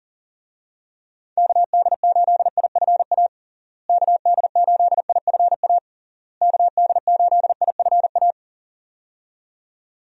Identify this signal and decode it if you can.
Morse code, heard as a keyed tone, about 40 wpm, 700 Hz. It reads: KD8IFA KD8IFA KD8IFA
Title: USA Call Signs - 6 Characters - Repeated 3x - 40wpm